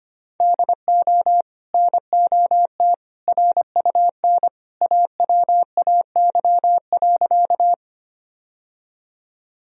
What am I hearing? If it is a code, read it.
Morse code, heard as a keyed tone, about 25 wpm, 700 Hz. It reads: DO NOT RUN AWAY.
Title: Sentences from Top 1000 Words - Review Part 1 - 25wpm